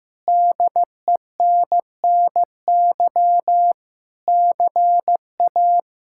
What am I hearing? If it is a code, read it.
Morse code, heard as a keyed tone, about 15 wpm, 700 Hz. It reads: DENNY CA
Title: K1USN SST QSO Practice - 15wpm